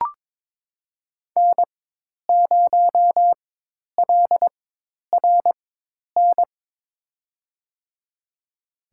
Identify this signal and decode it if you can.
Morse code, heard as a keyed tone, about 22 wpm, spaced as for 10 wpm, 700 Hz. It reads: N0LRN
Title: USA Call Signs - 5 Characters - 4x Character Spacing - 22wpm